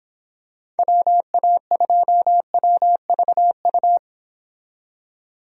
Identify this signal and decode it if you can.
Morse code, heard as a keyed tone, about 26 wpm, 700 Hz. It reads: WA2WVU